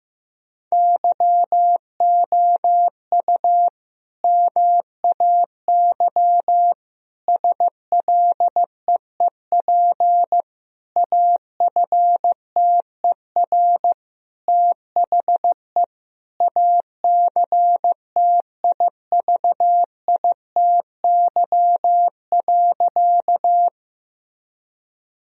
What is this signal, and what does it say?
Morse code, heard as a keyed tone, about 16 wpm, 700 Hz. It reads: YOU MAY SLEEP AFTER THE ACTIVITY.
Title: Sentences from Top 1000 Words - Review Part 4 - 15wpm